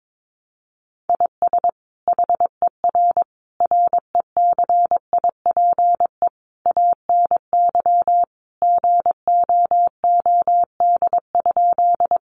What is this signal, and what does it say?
Morse code, heard as a keyed tone, about 22 wpm, 700 Hz. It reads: IS HER RECIPE ANY GOOD?
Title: Sentences from Top 1200 Words 22wpm